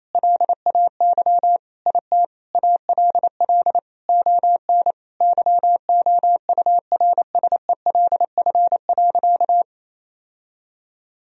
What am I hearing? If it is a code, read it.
Morse code, heard as a keyed tone, about 27 wpm, 700 Hz. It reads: LAY IT ALL ON YOURSELF.
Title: Sentences from Top 1000 Words - Review Part 2 - 28wpm